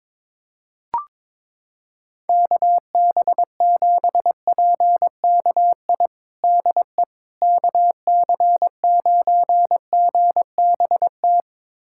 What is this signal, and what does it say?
Morse code, heard as a keyed tone, about 22 wpm, 700 Hz. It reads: KB7PKI DE KC9GBT
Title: USA Callsign DE Callsign - 6 Characters - 22wpm